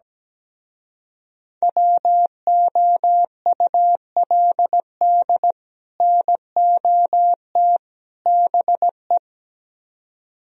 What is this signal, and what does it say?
Morse code, heard as a keyed tone, about 17 wpm, 700 Hz. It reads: WOULD NOT BE
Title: Speed-Racing - Sets of 3 Words 17wpm